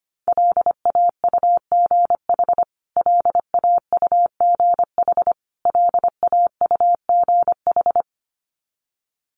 Morse code, heard as a keyed tone, about 25 wpm, 700 Hz. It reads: LAUGH LAUGH LAUGH
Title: Top 500 Words Repeated 25wpm